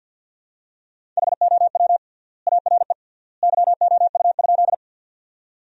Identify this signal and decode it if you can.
Morse code, heard as a keyed tone, about 47 wpm, 700 Hz. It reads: HOW ARE YOU?
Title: 3x Word-Spacing - Sentences from Top 100 Words - 50wpm